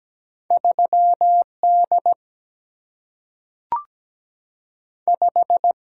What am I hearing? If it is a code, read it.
Morse code, heard as a keyed tone, about 17 wpm, 700 Hz. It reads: 3 D 5
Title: Single-Letter Single-Number 17wpm